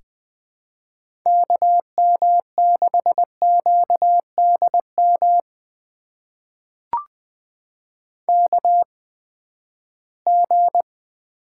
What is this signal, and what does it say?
Morse code, heard as a keyed tone, about 20 wpm, spaced as for 14 wpm, 700 Hz. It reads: KM6QDM KG
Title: USA Call Signs - 6 Characters - 8x Character Spacing - 20wpm